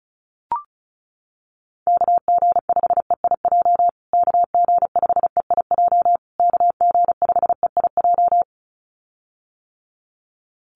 Morse code, heard as a keyed tone, about 34 wpm, 700 Hz. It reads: KG5EIJ KG5EIJ KG5EIJ
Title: USA Call Signs Repeated 3x 35wpm